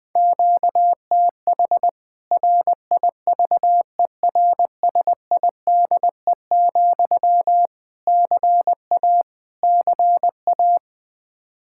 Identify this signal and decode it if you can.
Morse code, heard as a keyed tone, about 20 wpm, 700 Hz. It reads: QTH RIVERSIDE, CA CA
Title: K1USN SST QSO Practice 20wpm